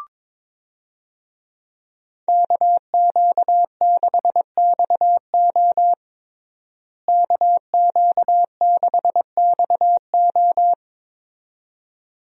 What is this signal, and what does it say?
Morse code, heard as a keyed tone, about 22 wpm, 700 Hz. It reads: KQ6XO KQ6XO
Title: USA Call Signs - 5 Characters - Repeated 2x - 22wpm